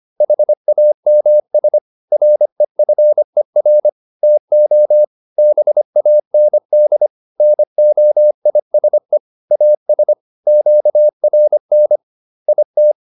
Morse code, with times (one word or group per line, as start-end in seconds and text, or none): 0.2-1.8 HAMS
2.1-3.9 REFER
4.2-5.0 TO
5.4-7.1 BAND
7.4-9.2 NOISE
9.5-10.1 AS
10.5-12.0 QRN
12.5-12.9 IT